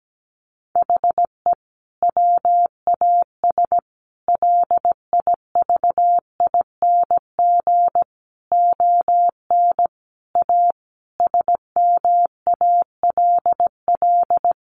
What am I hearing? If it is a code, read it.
Morse code, heard as a keyed tone, about 17 wpm, 700 Hz. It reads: HE WAS LIVING ON A SMALL
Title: Sentences from Top 1200 Words 17wpm